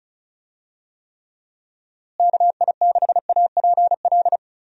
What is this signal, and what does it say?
Morse code, heard as a keyed tone, about 36 wpm, 700 Hz. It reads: KI6APL
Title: USA Call Signs - 6 Characters - Repeated 3x - 35wpm